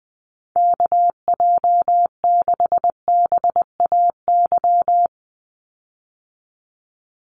Morse code, heard as a keyed tone, about 20 wpm, 700 Hz. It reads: KJ6BAY
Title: USA Call Signs 20wpm